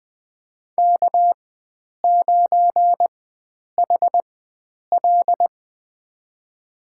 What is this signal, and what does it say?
Morse code, heard as a keyed tone, about 21 wpm, spaced as for 9 wpm, 700 Hz. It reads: K9HL
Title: USA Call Signs - 4 Characters - 4x Character Spacing - 20wpm